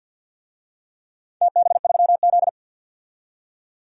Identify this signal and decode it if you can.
Morse code, heard as a keyed tone, about 52 wpm, 700 Hz. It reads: TB3Z
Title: DXCC-100 Call Signs - 4 Characters - 50wpm